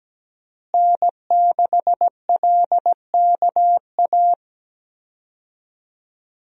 Morse code, heard as a keyed tone, about 17 wpm, 700 Hz. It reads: N6LKA
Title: USA Call Signs - 5 Characters - Repeated 3x - 17wpm